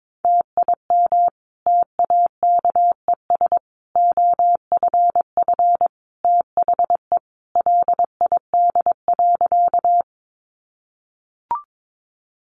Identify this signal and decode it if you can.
Morse code, heard as a keyed tone, about 22 wpm, 700 Hz. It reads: TIM TAKES OFF THE LID.
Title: Easy Phrases 22wpm